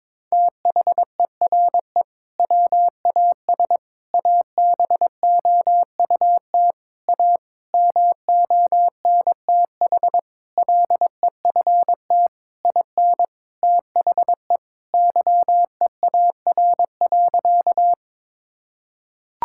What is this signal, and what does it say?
Morse code, heard as a keyed tone, about 21 wpm, 700 Hz. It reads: THERE WAS ABOUT A MONTH LEFT IN THE YEAR.
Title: Sentences from Top 1000 Words - Review Part 5 - 22wpm